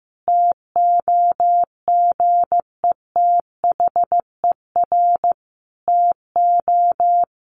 Morse code, heard as a keyed tone, about 15 wpm, 700 Hz. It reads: TOGETHER TO
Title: Sentences from Top 1100 Words 15wpm